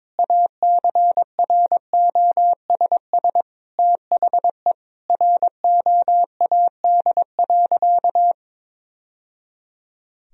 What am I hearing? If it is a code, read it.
Morse code, heard as a keyed tone, about 22 wpm, 700 Hz. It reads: ACROSS THE ROAD.
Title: Sentences from Top 1000 Words - Review Part 1 - 22wpm